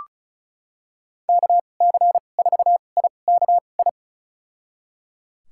Morse code, heard as a keyed tone, about 36 wpm, spaced as for 25 wpm, 700 Hz. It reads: KC4IKI
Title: USA Call Signs - 2x Character Spacing - 35wpm